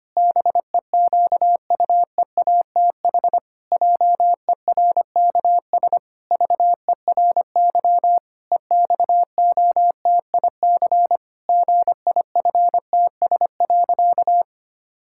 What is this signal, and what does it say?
Morse code, heard as a keyed tone, about 25 wpm, 700 Hz. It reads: BEQUEATH JERKS VERY EXOTIC GIFTS.
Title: Pangram Challenge v2 - 25wpm